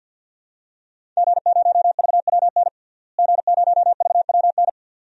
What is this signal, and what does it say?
Morse code, heard as a keyed tone, about 49 wpm, 700 Hz. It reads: G0VWN G0VWN
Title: DXCC-100 Call Signs - 5 Characters - Repeated 2x - 50wpm